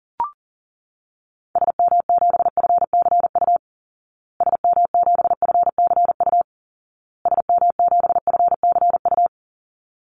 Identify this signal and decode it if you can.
Morse code, heard as a keyed tone, about 40 wpm, 700 Hz. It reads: SM7FCU SM7FCU SM7FCU
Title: Call Signs Repeated 2x - 40wpm